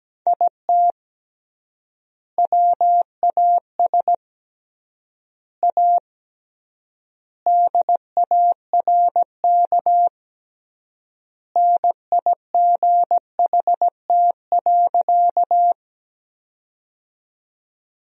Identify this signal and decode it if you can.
Morse code, heard as a keyed tone, about 17 wpm, 700 Hz. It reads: IT WAS A DARK NIGHT.
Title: 3x Word-Spacing - Sentences from Top 500 Words - 17wpm